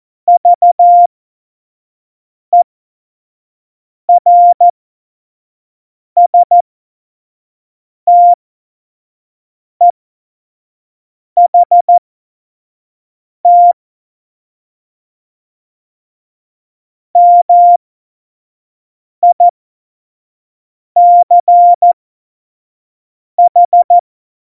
Morse code, heard as a keyed tone, about 14 wpm, spaced as for 5 wpm, 700 Hz. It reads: VERSTEHT MICH
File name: Morsekurs / Witze / Witz 26. November 2019, 25 F70